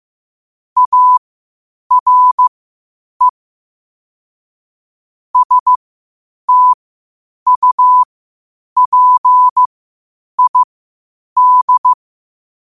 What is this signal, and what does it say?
Morse code, heard as a keyed tone, about 15 wpm, spaced as for 8 wpm, 1000 Hz. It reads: ARE STUPID
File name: qotdfiles / 8wpm